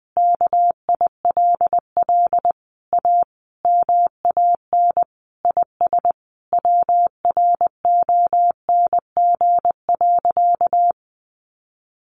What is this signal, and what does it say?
Morse code, heard as a keyed tone, about 20 wpm, 700 Hz. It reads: KILL A MAN IS WRONG.
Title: Sentences from Top 1400 Words 20wpm